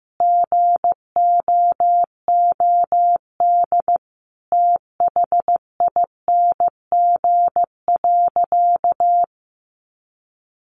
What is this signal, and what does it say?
Morse code, heard as a keyed tone, about 15 wpm, 700 Hz. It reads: GOOD THING.
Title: Sentences from Top 300 Words 15wpm